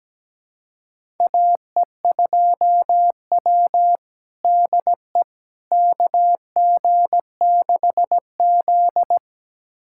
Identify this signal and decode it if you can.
Morse code, heard as a keyed tone, about 17 wpm, 700 Hz. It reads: AE2W DE KG6Z
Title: USA Callsign DE Callsign - 4 Characters - 17wpm